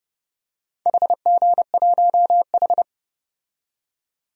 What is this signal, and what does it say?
Morse code, heard as a keyed tone, about 30 wpm, 700 Hz. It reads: HG1H
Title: DXCC-100 Call Signs - 4 Characters - 30wpm